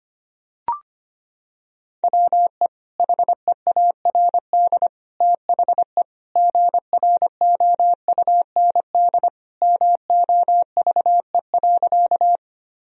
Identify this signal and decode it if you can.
Morse code, heard as a keyed tone, about 25 wpm, 700 Hz. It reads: WE HEARD THE GROUND MOVE.
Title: Sentences from Top 1000 Words - Review Part 2 - 25wpm